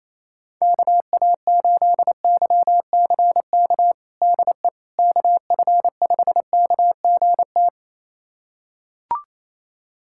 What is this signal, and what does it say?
Morse code, heard as a keyed tone, about 28 wpm, 700 Hz. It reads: KA8YCK DE KF5KGT
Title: USA Callsign DE Callsign - 6 Characters - 28wpm